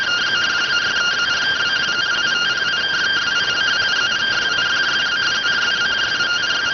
Two-tone teleprinter signal, about 170 Hz shift.